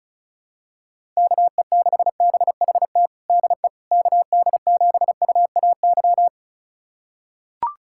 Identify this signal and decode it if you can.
Morse code, heard as a keyed tone, about 36 wpm, 700 Hz. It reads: KE6BHT DE KD7UAY
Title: USA Callsign DE Callsign 35wpm